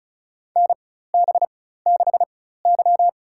Morse code, heard as a keyed tone, about 36 wpm, spaced as for 16 wpm, 700 Hz. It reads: NB6Y
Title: USA Call Signs - 4 Characters - 4x Character Spacing - 35wpm